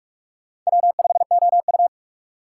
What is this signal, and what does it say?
Morse code, heard as a keyed tone, about 46 wpm, 700 Hz. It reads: W5OU
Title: USA Call Signs - 4 Characters - 4x Character Spacing - 45wpm